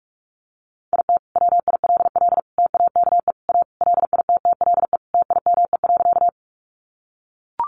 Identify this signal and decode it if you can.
Morse code, heard as a keyed tone, about 44 wpm, 700 Hz. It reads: IT WILL TAKE A LITTLE TIME.
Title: Sentences from Top 200 Words 45wpm